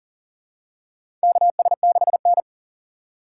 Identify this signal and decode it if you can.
Morse code, heard as a keyed tone, about 40 wpm, 690 Hz. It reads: KS6N
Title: USA Call Signs - 4 Characters - 8x Character Spacing - 40wpm